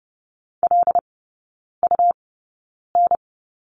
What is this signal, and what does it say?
Morse code, heard as a keyed tone, about 30 wpm, spaced as for 9 wpm, 700 Hz. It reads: LUN
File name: Lesson 19e - Letter w - Three Characters 30wpm